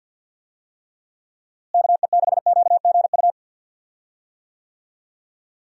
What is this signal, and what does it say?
Morse code, heard as a keyed tone, about 52 wpm, 700 Hz. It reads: KE6QGU
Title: USA Call Signs - 6 Characters - Repeated 3x - 50wpm